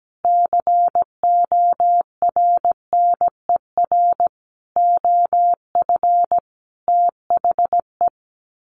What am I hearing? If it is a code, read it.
Morse code, heard as a keyed tone, about 17 wpm, 700 Hz. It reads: CORNER OF THE
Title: Sets of 5 Words Encore 17wpm